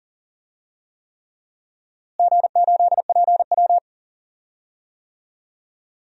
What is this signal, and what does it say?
Morse code, heard as a keyed tone, about 40 wpm, 700 Hz. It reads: G8PW
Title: DXCC-100 Call Signs - 4 Characters - Repeated 3x - 40wpm